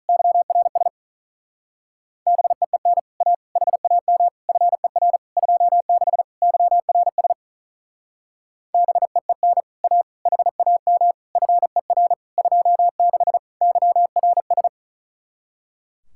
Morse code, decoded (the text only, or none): YRS BEEN A HAM FER 26 YRS BEEN A HAM FER 26 YRS